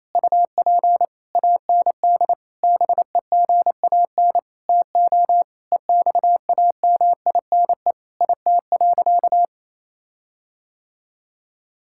Morse code, heard as a keyed tone, about 28 wpm, 700 Hz. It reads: UP AND BEGAN TO EXAMINE IT.
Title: Sentences from Top 1600 Words 28wpm